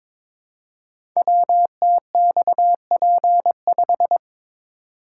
Morse code, heard as a keyed tone, about 22 wpm, 700 Hz. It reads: WTXP5
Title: Code Groups - 5 Letter:Numbers - 22wpm